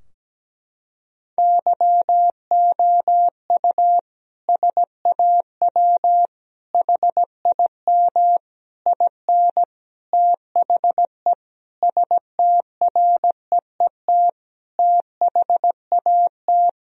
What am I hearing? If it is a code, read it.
Morse code, heard as a keyed tone, about 17 wpm, 700 Hz. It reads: YOU SAW HIM IN THE STREET THAT
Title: Sentences from Top 1000 Words - Review Part 3 - 17wpm